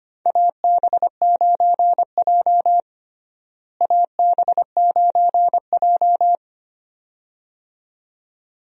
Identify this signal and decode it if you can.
Morse code, heard as a keyed tone, about 25 wpm, 700 Hz. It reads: AB9J AB9J
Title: USA Call Signs - 4 Characters - Repeated 2x - 25wpm